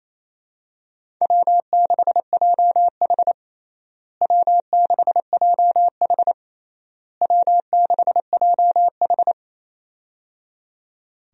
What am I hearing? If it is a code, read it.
Morse code, heard as a keyed tone, about 27 wpm, 700 Hz. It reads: W6JH W6JH W6JH
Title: USA Call Signs - 4 Characters - Repeated 3x - 28wpm